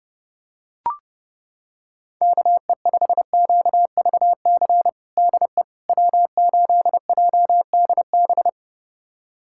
Morse code, heard as a keyed tone, about 30 wpm, 700 Hz. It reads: KE5QVC DE W8JDB
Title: USA Callsign DE Callsign 30wpm